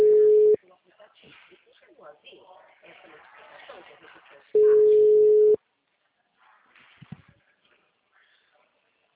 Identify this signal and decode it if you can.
Ringback tone (Europe, 425 Hz), with speech.
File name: out- -2019- - - .18014